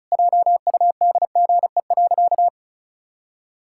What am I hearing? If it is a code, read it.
Morse code, heard as a keyed tone, about 34 wpm, 700 Hz. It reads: JUDGE.
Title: Pangram Challenge v2 - 35wpm